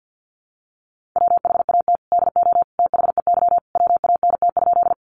Morse code, heard as a keyed tone, about 50 wpm, 700 Hz. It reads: WHAT DO THEY WANT?
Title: Sentences from Top 200 Words 50wpm